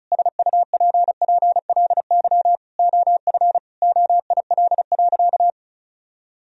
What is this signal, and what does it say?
Morse code, heard as a keyed tone, about 36 wpm, 700 Hz. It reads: SUPPLY OF OIL.
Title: Sentences from Top 1000 Words - Review Part 4 - 35wpm